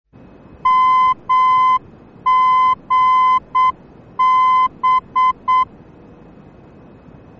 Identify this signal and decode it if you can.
Morse code, heard as a keyed tone, about 7 wpm, 1020 Hz. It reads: MGB
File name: germany / mx / MGB-D